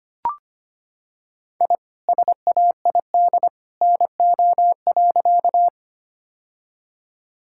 Morse code, heard as a keyed tone, about 25 wpm, 700 Hz. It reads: I SAID NO.
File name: practice / 25wpm / Sentences with 2-Word Phrases - 25wpm